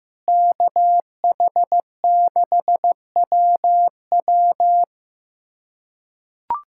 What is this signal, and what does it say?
Morse code, heard as a keyed tone, about 16 wpm, 700 Hz. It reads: KH6WW
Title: USA Callsign DE Callsign - 5 Characters - 15wpm